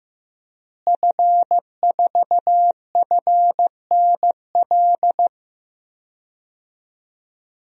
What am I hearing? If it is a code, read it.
Morse code, heard as a keyed tone, about 15 wpm, 700 Hz. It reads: F4FNL
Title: DXCC-100 Call Signs - 5 Characters - Repeated 3x - 15wpm